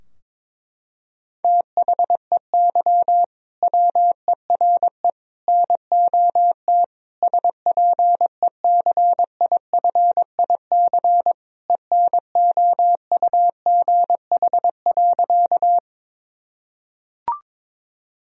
Morse code, heard as a keyed tone, about 22 wpm, 700 Hz. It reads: THEY WERE NOT SPECIFIC ENOUGH.
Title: Sentences from Top 1000 Words - Review Part 3 - 22wpm